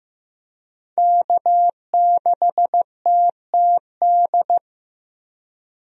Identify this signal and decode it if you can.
Morse code, heard as a keyed tone, about 15 wpm, 700 Hz. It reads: K6TTD
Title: USA Call Signs - 5 Characters - Repeated 2x - 15wpm